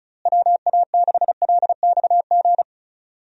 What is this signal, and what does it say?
Morse code, heard as a keyed tone, about 36 wpm, 700 Hz. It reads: WA6LXG
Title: USA Call Signs - 6 Characters - Repeated 3x - 35wpm